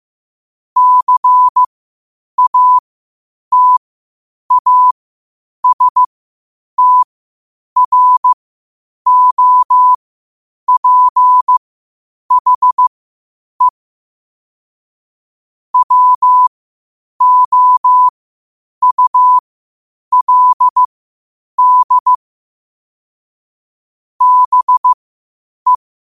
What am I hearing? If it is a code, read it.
Morse code, heard as a keyed tone, about 15 wpm, spaced as for 8 wpm, 1000 Hz. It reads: CATASTROPHE WOULD BE